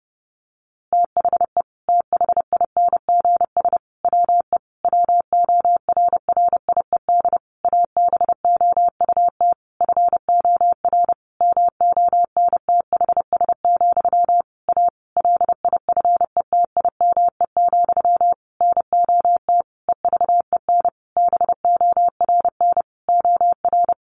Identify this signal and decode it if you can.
Morse code, heard as a keyed tone, about 30 wpm, 700 Hz. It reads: THE THINGS WE WORRIED ABOUT FOR MONTHS, A LIFETIME, NOT EVEN BORN OR
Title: A Sound Of Thunder 30wpm